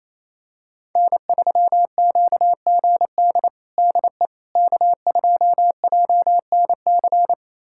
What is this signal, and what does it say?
Morse code, heard as a keyed tone, about 28 wpm, 700 Hz. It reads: N3QGD DE K2JNC
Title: USA Callsign DE Callsign - 5 Characters - 28wpm